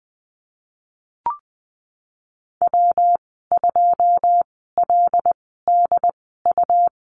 Morse code, heard as a keyed tone, about 20 wpm, 700 Hz. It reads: W 2 L D U
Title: 154 Lesson 25g - Letter p - Call Signs 20wpm